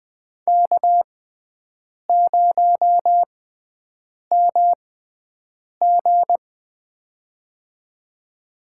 Morse code, heard as a keyed tone, about 20 wpm, spaced as for 7 wpm, 700 Hz. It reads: K0MG
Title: USA Call Signs - 6x Character Spacing - 20wpm